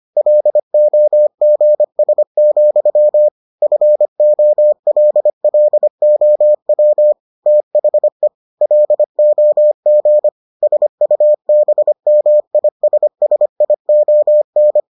Morse code, heard as a keyed tone, about 25 wpm, 600 Hz. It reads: LOGS, FOLLOW THE LOG SUBMISSION